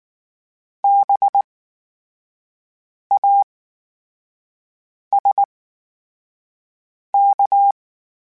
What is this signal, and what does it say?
Morse code, heard as a keyed tone, about 19 wpm, spaced as for 5 wpm, 800 Hz. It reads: BASK
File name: Special INGs / Special INGs-mp3s / VerySlowToFaster Ing @18